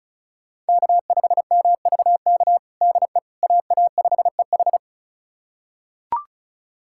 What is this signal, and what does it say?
Morse code, heard as a keyed tone, about 36 wpm, 690 Hz. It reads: K5MVK DE AA5EH
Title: USA Callsign DE Callsign - 5 Characters - 35wpm